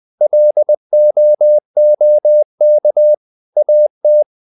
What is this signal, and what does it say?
Morse code, heard as a keyed tone, about 20 wpm, 600 Hz. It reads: LOOK AT
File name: the old man and the sea part 2 20wpm